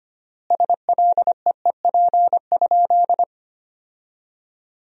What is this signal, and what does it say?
Morse code, heard as a keyed tone, about 25 wpm, 700 Hz. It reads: SLEEP?
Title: 3x Word-Spacing - Sentences from Top 500 Words - 25wpm